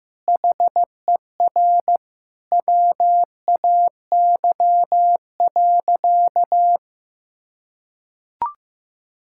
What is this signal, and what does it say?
Morse code, heard as a keyed tone, about 15 wpm, 700 Hz. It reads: HER WAY.
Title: Sentences from Top 1600 Words 15wpm